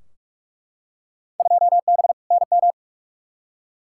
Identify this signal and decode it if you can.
Morse code, heard as a keyed tone, about 43 wpm, 700 Hz. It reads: 2B NM
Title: Field Day Exchanges Encore - 45wpm